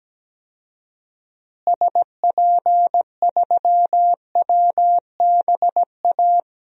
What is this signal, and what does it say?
Morse code, heard as a keyed tone, about 17 wpm, 700 Hz. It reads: SP3WBA
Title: DXCC-100 Call Signs - 6 Characters - Repeated 2x - 17wpm